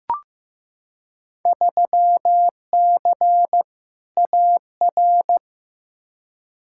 Morse code, heard as a keyed tone, about 15 wpm, 700 Hz. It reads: 3C AR
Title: Field Day Exchanges Encore - 15wpm